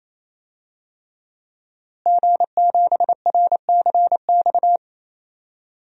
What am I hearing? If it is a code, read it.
Morse code, heard as a keyed tone, about 28 wpm, 700 Hz. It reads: G7RCX
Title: DXCC-100 Call Signs - 5 Characters - Repeated 3x - 28wpm